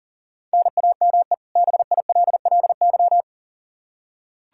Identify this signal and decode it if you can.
Morse code, heard as a keyed tone, about 37 wpm, 700 Hz. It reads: NAME BILLY
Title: Practice QSO - Exchange 1 - Beginning - 40wpm